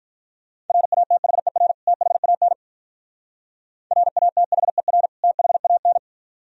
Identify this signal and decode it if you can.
Morse code, heard as a keyed tone, about 45 wpm, 700 Hz. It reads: RATHER THAN RATHER THAN